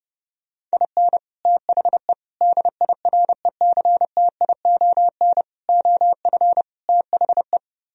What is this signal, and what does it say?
Morse code, heard as a keyed tone, about 29 wpm, 700 Hz. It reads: IN THE DIRECTION OF THE